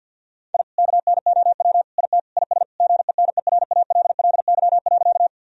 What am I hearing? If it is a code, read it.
Morse code, heard as a keyed tone, about 50 wpm, 700 Hz. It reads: I KNOW IT IS GENERALLY.
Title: Sentences from Top 1000 Words - Review Part 4 - 50wpm